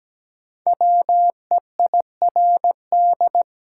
Morse code, heard as a keyed tone, about 17 wpm, 700 Hz. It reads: WEIRD
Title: Five Letter Words - 17wpm